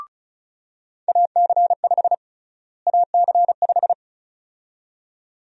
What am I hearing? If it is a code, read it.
Morse code, heard as a keyed tone, about 34 wpm, 700 Hz. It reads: AC5 AC5
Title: USA Call Sign Prefixes - Repeated 2x - 35wpm